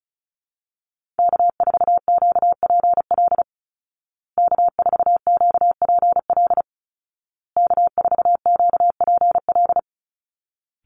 Morse code, heard as a keyed tone, about 34 wpm, 700 Hz. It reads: K4QPL K4QPL K4QPL
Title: Call Signs Repeated 2x - 35wpm